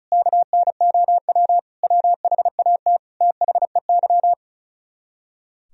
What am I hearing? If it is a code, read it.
Morse code, heard as a keyed tone, about 46 wpm, spaced as for 41 wpm, 700 Hz. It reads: KNOW WHAT THEY
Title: Speed-Racing - Sets of 5 Words 35wpm